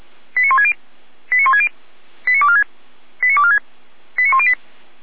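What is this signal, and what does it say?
Five-tone selective call (ZVEI), heard as quick runs of five tones, tones of about 70 ms. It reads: 89160 89160 89246 89246 89108
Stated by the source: ZVEI five-tone selective call received on an SDR.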